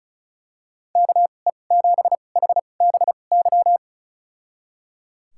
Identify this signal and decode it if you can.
Morse code, heard as a keyed tone, about 34 wpm, spaced as for 25 wpm, 700 Hz. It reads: KE7HBY
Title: USA Call Signs - 2x Character Spacing - 35wpm